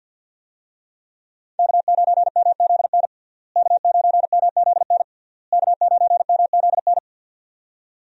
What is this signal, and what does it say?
Morse code, heard as a keyed tone, about 48 wpm, 700 Hz. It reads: K9MZN K9MZN K9MZN
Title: USA Call Signs - 5 Characters - Repeated 3x - 50wpm